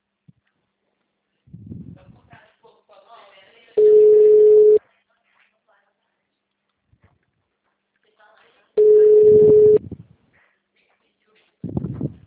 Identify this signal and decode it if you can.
Ringback tone (Europe, 425 Hz), with speech.
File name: out- -2009- - - .29